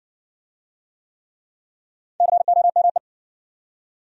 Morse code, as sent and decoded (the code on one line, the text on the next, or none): -.-. --- -- .
COME